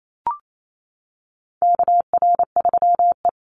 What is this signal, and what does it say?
Morse code, heard as a keyed tone, about 28 wpm, 700 Hz. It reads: KR3E
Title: Call Signs - 28wpm